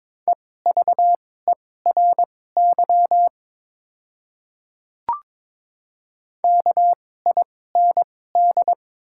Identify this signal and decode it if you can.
Morse code, heard as a keyed tone, about 22 wpm, spaced as for 16 wpm, 700 Hz. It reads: EVERY KIND
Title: 2x Character Spacing - Top 500 Words - 22wpm